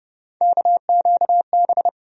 Morse code, heard as a keyed tone, about 32 wpm, 700 Hz. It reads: KQB
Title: USA Call Sign Postfixes - 3 Letters 30wpm